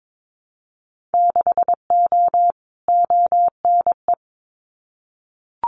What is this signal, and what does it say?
Morse code, heard as a keyed tone, about 22 wpm, 700 Hz. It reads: 6O ONE
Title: Winter Field Day Exchanges Encore 22wpm